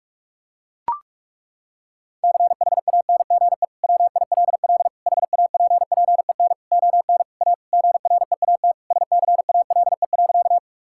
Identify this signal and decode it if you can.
Morse code, heard as a keyed tone, about 41 wpm, 700 Hz. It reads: CHANGE WILL HAPPEN ON A GREAT SCALE.
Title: Sentences from Top 1000 Words - Review Part 5 - 45wpm